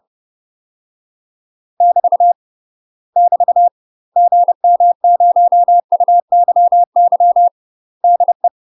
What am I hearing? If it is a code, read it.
Morse code, heard as a keyed tone, about 30 wpm, 700 Hz. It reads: <BT> <BT> GM0UYY DE
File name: UK MASTER QSOs TRAINING / 30WPM / UK MASTER QSOs 0008